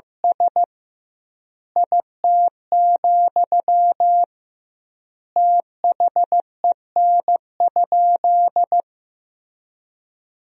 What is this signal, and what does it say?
Morse code, heard as a keyed tone, about 16 wpm, 700 Hz. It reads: S IT, THEN?
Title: 2x Word-Spacing - Sentences from Top 200 Words - 15wpm